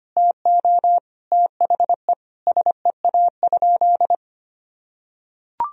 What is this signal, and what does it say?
Morse code, heard as a keyed tone, about 25 wpm, 700 Hz. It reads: TO THE SEA?
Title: Sentences from Top 1000 Words - Review Part 1 - 25wpm